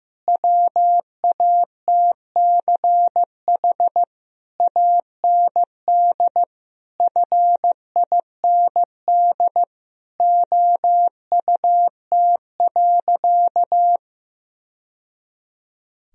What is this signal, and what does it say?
Morse code, heard as a keyed tone, about 16 wpm, 700 Hz. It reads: WATCH AND FIND OUT.
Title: Sentences from Top 1000 Words - Review Part 1 - 15wpm